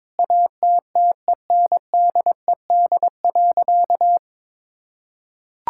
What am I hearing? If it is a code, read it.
Morse code, heard as a keyed tone, about 22 wpm, 700 Hz. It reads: ATTENDED.
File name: practice / 22wpm / Sentences with 4-Word Phrases - 22wpm